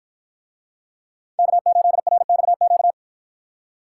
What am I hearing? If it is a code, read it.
Morse code, heard as a keyed tone, about 49 wpm, 700 Hz. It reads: K8RXQ